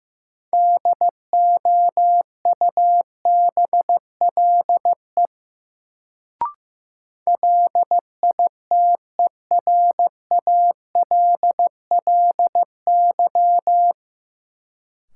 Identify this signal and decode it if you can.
Morse code, heard as a keyed tone, about 15 wpm, 700 Hz. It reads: DOUBLE LITERALLY
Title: Sentences from Top 1400 Words - New Word Drill - 15wpm